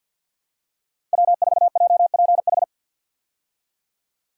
Morse code, heard as a keyed tone, about 52 wpm, 700 Hz. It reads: W4JPH
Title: K1USN SST QSO Practice - 50wpm